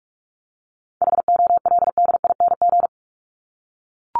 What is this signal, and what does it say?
Morse code, heard as a keyed tone, about 44 wpm, 700 Hz. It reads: HOLDING
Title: ING Suffix - Top 100 Words with Base Words 45wpm